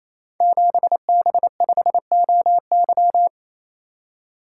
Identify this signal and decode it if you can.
Morse code, heard as a keyed tone, about 29 wpm, 700 Hz. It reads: 7B5OY
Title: Code Groups - 5 Letter:Numbers - 28wpm